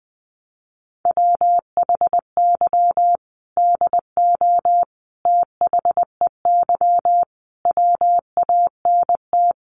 Morse code, heard as a keyed tone, about 20 wpm, 700 Hz. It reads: WHY DO THEY WANT
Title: Sentences from Top 400 Words 20wpm